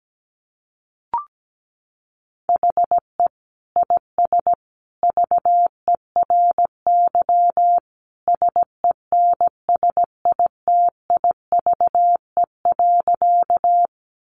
Morse code, heard as a keyed tone, about 17 wpm, 700 Hz. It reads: HE IS VERY SENSITIVE.
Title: Sentences from Top 1400 Words 17wpm